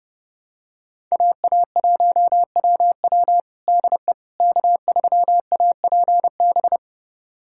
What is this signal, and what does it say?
Morse code, heard as a keyed tone, about 30 wpm, 700 Hz. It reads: AA1WW DE K3APB
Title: USA Callsign DE Callsign - 5 Characters - 30wpm